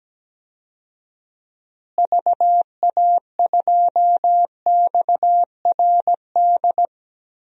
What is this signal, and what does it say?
Morse code, heard as a keyed tone, about 17 wpm, 700 Hz. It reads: VA2XRD
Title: DXCC-100 Call Signs - 6 Characters - Repeated 3x - 17wpm